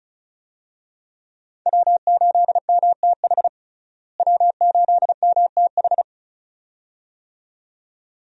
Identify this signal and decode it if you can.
Morse code, heard as a keyed tone, about 34 wpm, 700 Hz. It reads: W8MTH W8MTH
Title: USA Call Signs - 5 Characters - Repeated 2x - 35wpm